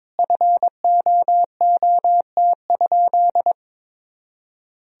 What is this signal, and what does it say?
Morse code, heard as a keyed tone, about 23 wpm, 700 Hz. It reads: FOOT?
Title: Sentences from Top 500 Words and Abbreviations 22wpm